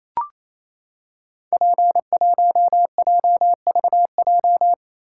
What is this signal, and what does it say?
Morse code, heard as a keyed tone, about 27 wpm, 700 Hz. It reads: P1JVJ
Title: Code Groups - 5 Letter:Numbers - 28wpm